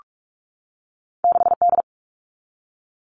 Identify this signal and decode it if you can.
Morse code, heard as a keyed tone, about 46 wpm, 700 Hz. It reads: 6D
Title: All Cut Numbers - 2-Digit 45wpm